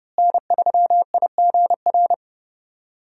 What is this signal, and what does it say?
Morse code, heard as a keyed tone, about 32 wpm, 700 Hz. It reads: N3IGR
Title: USA Call Signs - 5 Characters - Repeated 3x - 30wpm